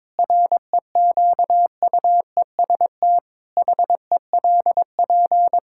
Morse code, heard as a keyed tone, about 22 wpm, 700 Hz. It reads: REQUEST HELP
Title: Pangram Challenge v2 - 22wpm